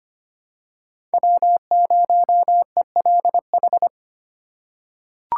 Morse code, heard as a keyed tone, about 25 wpm, 700 Hz. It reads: W0ELH
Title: USA Call Signs - 5 Characters - 6x Character Spacing - 25wpm